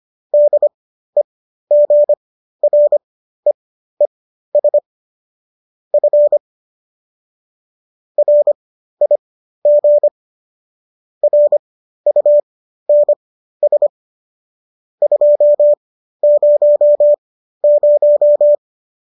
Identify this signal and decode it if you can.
Morse code, heard as a keyed tone, about 25 wpm, spaced as for 13 wpm, 590 Hz. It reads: DEGREES F RIG RUNS 200